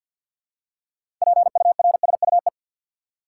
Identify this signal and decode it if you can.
Morse code, heard as a keyed tone, about 52 wpm, 700 Hz. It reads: PURSUE
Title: Six Letter Words - 50wpm